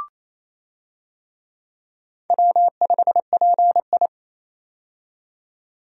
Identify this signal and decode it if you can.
Morse code, heard as a keyed tone, about 28 wpm, 700 Hz. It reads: W5PI
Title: USA Call Signs - 4 Characters - 28wpm